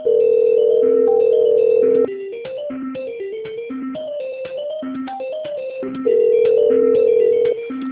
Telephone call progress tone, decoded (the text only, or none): ringback tone